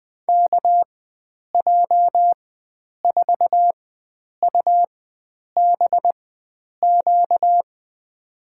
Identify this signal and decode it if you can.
Morse code, heard as a keyed tone, about 20 wpm, spaced as for 9 wpm, 700 Hz. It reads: KJ4UBQ
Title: USA Call Signs - 6 Characters - 4x Character Spacing - 20wpm